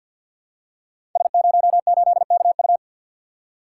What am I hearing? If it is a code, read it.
Morse code, heard as a keyed tone, about 50 wpm, 700 Hz. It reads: S08KU
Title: Code Groups - 5 Letter:Numbers - 50wpm